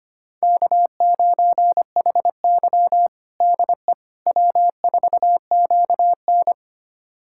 Morse code, heard as a keyed tone, about 25 wpm, 700 Hz. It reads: K9HY DE W4QN
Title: USA Callsign DE Callsign - 4 Characters - 25wpm